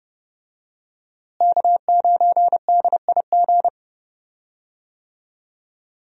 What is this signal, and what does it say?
Morse code, heard as a keyed tone, about 30 wpm, 700 Hz. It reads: K9DIG
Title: USA Call Signs - 5 Characters - Repeated 3x - 30wpm